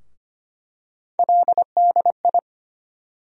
Morse code, heard as a keyed tone, about 25 wpm, 700 Hz. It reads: LDI
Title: USA Call Sign Postfixes - 3 Letters 25wpm